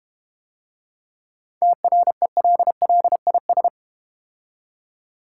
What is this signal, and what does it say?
Morse code, heard as a keyed tone, about 31 wpm, 700 Hz. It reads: TRELLIS
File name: Easy Words - Speed-Racing with Hints - 25wpm